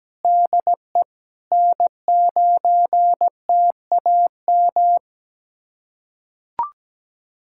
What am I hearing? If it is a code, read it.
Morse code, heard as a keyed tone, about 17 wpm, 700 Hz. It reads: DE N9TAM
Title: USA Callsign DE Callsign - 5 Characters - 17wpm